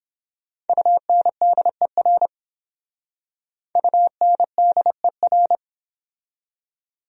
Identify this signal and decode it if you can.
Morse code, heard as a keyed tone, about 26 wpm, 700 Hz. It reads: UNDER UNDER
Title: Speed-Racing Top 500 Words 22wpm